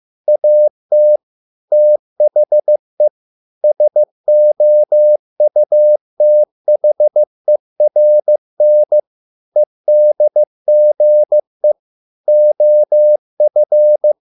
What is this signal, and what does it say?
Morse code, heard as a keyed tone, about 15 wpm, 600 Hz. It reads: AT THE SOUTHERN EDGE OF